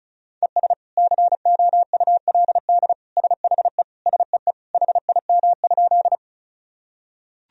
Morse code, heard as a keyed tone, about 36 wpm, 700 Hz. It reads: ES COULD SHE SEE HIM?
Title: Sentences from Top 100 Words and Abbreviations 35wpm